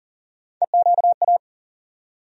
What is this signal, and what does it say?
Morse code, heard as a keyed tone, about 40 wpm, 700 Hz. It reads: EQA
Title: USA Call Sign Postfixes - 3 Letters 40wpm